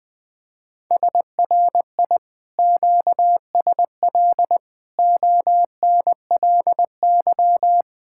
Morse code, heard as a keyed tone, about 21 wpm, 700 Hz. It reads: SRI QSL ONLY
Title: Speed-Racing Common QSO Phrases - 20wpm